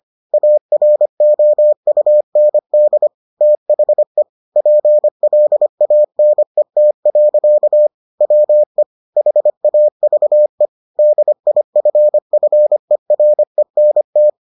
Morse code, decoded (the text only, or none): AROUND THE PLANET. WE HAVE DIFFERENT